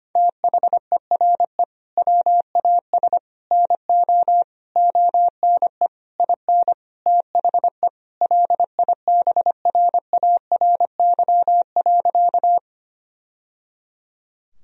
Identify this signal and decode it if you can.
Morse code, heard as a keyed tone, about 25 wpm, 700 Hz. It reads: THERE WAS NO ONE IN THE LIBRARY.
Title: Sentences from Top 1000 Words - Review Part 4 - 25wpm